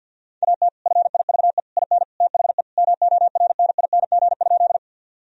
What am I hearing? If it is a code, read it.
Morse code, heard as a keyed tone, about 51 wpm, 700 Hz. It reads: AT FIVE IN THE MORNING?
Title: Sentences from Top 1000 Words - Review Part 2 - 50wpm